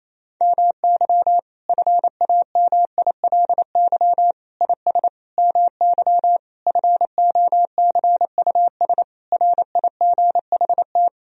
Morse code, heard as a keyed tone, about 28 wpm, 700 Hz. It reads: MY FAMILY IS MY FOCUS RIGHT
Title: Sentences from Top 1000 Words - Review Part 4 - 28wpm